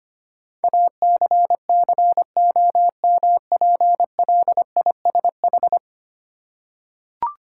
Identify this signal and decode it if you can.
Morse code, heard as a keyed tone, about 25 wpm, 700 Hz. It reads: ACCOMPLISH
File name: 10 Letter Words - First 3 Letters Given - 25wpm